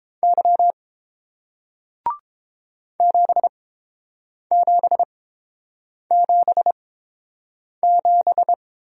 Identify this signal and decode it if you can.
Morse code, heard as a keyed tone, about 30 wpm, spaced as for 7 wpm, 700 Hz. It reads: Y 7777